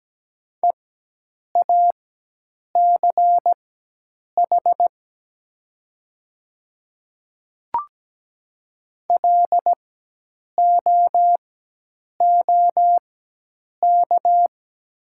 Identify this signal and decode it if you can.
Morse code, heard as a keyed tone, about 17 wpm, spaced as for 8 wpm, 700 Hz. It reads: EACH LOOK
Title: 4x Character Spacing - Top 100 Words - 17wpm